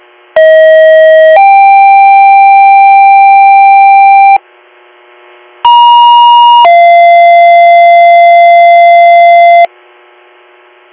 Two-tone page, heard as a one-second tone, then a three-second tone, about 640 then 790 Hz, about 950 then 690 Hz. It is heard from 0.4 to 9.7 s.